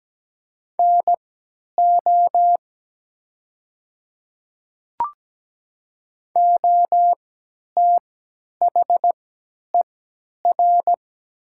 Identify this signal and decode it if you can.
Morse code, heard as a keyed tone, about 17 wpm, spaced as for 10 wpm, 700 Hz. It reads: NO OTHER
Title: 3x Character Spacing - Top 100 Words - 17wpm